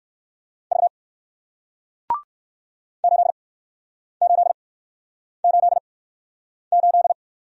Morse code, heard as a keyed tone, about 59 wpm, spaced as for 9 wpm, 700 Hz. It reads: V 8888